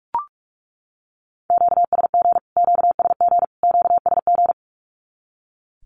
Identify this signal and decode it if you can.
Morse code, heard as a keyed tone, about 44 wpm, 700 Hz. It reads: QSG QSG QSG
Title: Q-Codes Repeated 3x 45wpm